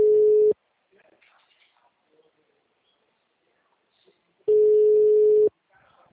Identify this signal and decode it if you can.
Ringback tone (Europe, 425 Hz), with speech.